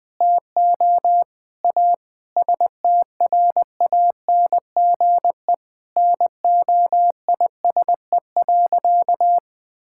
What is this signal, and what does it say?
Morse code, heard as a keyed tone, about 20 wpm, 700 Hz. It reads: TO A STRANGE NOISE.
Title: Sentences with 8-Word Phrases - 20wpm